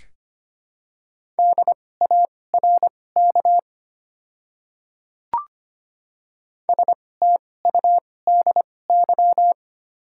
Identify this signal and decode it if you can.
Morse code, heard as a keyed tone, about 25 wpm, spaced as for 18 wpm, 700 Hz. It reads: DARK STUDY
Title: 2x Character Spacing - Top 500 Words - 25wpm